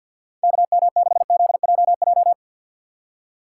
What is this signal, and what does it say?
Morse code, heard as a keyed tone, about 61 wpm, 700 Hz. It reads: KM6ZJJ
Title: Speed Racing - USA 2x3 Call Signs 50wpm